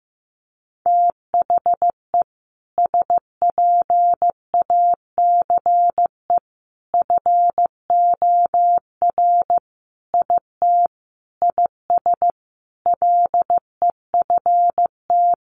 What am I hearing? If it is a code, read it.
Morse code, heard as a keyed tone, about 15 wpm, 700 Hz. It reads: THE SPACE FOR IT IS LEFT
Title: Sentences from Top 1500 Words 15wpm